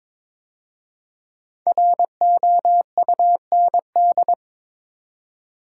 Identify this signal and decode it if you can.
Morse code, heard as a keyed tone, about 22 wpm, 700 Hz. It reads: ROUND
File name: 5 Letter Words - First 3 Letters Given - 22wpm